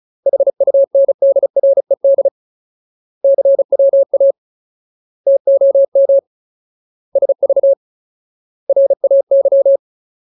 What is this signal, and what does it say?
Morse code, heard as a keyed tone, about 34 wpm, 550 Hz. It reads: HUNDRED CWA TOM SV RAY
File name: Ultimate Word List / 35WPM Ultimate